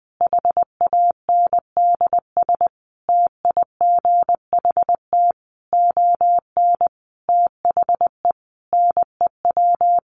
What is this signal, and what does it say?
Morse code, heard as a keyed tone, about 20 wpm, 700 Hz. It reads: HANDS TIGHT ON THE NEW
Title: A Sound Of Thunder 20wpm